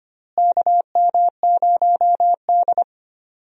Morse code, heard as a keyed tone, about 25 wpm, 700 Hz. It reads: KM0D
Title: USA Call Signs - 4 Characters - Repeated 3x - 25wpm